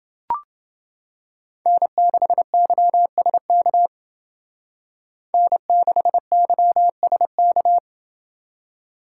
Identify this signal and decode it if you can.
Morse code, heard as a keyed tone, about 29 wpm, 700 Hz. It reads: N6YSK N6YSK